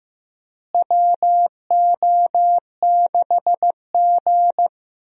Morse code, heard as a keyed tone, about 15 wpm, 700 Hz. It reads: WO6G
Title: USA Call Signs - 4 Characters - Repeated 3x - 15wpm